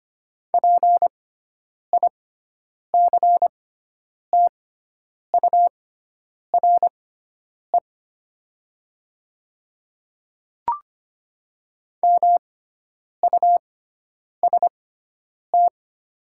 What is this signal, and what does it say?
Morse code, heard as a keyed tone, about 25 wpm, spaced as for 9 wpm, 700 Hz. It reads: PICTURE MUST
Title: 6x Character Spacing - Top 200 Words - 25wpm